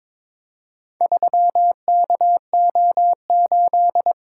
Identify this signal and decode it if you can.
Morse code, heard as a keyed tone, about 22 wpm, 700 Hz. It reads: 3KO8
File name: practice / 22wpm / Four Letter-Number Combinations - 22wpm